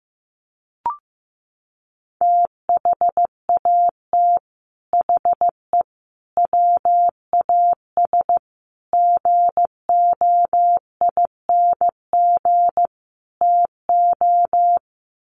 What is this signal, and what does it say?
Morse code, heard as a keyed tone, about 15 wpm, 700 Hz. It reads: THAT HE WAS GOING TO
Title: Sets of 5 Words 15wpm